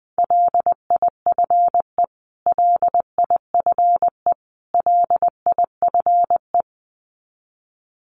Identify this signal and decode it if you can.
Morse code, heard as a keyed tone, about 20 wpm, 700 Hz. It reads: LIFE LIFE LIFE
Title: Top 500 Words Repeated 20wpm